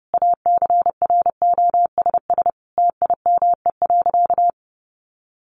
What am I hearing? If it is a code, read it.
Morse code, heard as a keyed tone, about 30 wpm, 700 Hz. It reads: ACROSS TIME.
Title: A Sound Of Thunder 30wpm